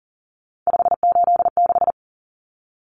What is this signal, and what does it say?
Morse code, heard as a keyed tone, about 40 wpm, 700 Hz. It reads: 586
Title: Numbers - 3-Digit 40wpm